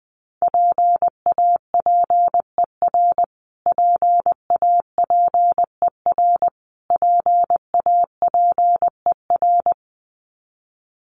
Morse code, heard as a keyed tone, about 20 wpm, 700 Hz. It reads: PAPER PAPER PAPER
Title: Top 500 Words Repeated 20wpm